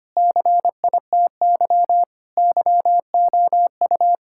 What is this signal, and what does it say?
Morse code, heard as a keyed tone, about 25 wpm, 700 Hz. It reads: CITY YOU
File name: practice / 25wpm / Sentences with 3-Word Phrases - 25wpm